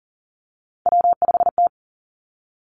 Morse code, heard as a keyed tone, about 42 wpm, 700 Hz. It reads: W5T
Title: Three Challenging Letter-Number Combinations 40wpm